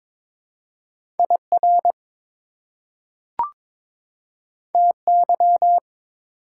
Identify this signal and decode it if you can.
Morse code, heard as a keyed tone, about 22 wpm, 700 Hz. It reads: IR TY
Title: Prefixes and Suffixes 22wpm